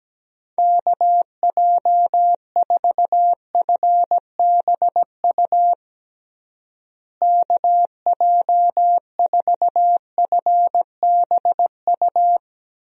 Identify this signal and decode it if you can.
Morse code, heard as a keyed tone, about 17 wpm, 700 Hz. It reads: KJ4FBU KJ4FBU
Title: USA Call Signs - 6 Characters - Repeated 2x - 17wpm